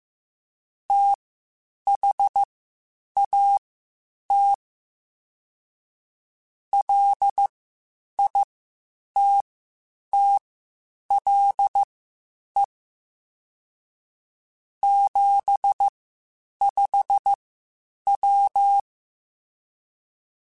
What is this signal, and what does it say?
Morse code, heard as a keyed tone, about 15 wpm, spaced as for 8 wpm, 800 Hz. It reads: THAT LITTLE 75W
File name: (spaced) / QSOs Spaced-mp3s / WA7QOX 1 @13 sp